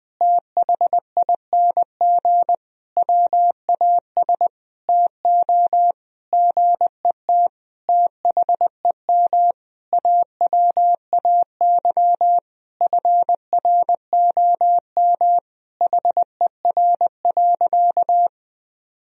Morse code, heard as a keyed tone, about 20 wpm, 700 Hz. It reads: THING WAS TO GET THEM AWAY FROM HER.
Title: Sentences from Top 1000 Words - Review Part 2 - 20wpm